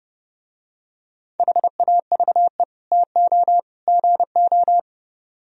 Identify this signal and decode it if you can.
Morse code, heard as a keyed tone, about 31 wpm, 700 Hz. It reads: HAVE TO GO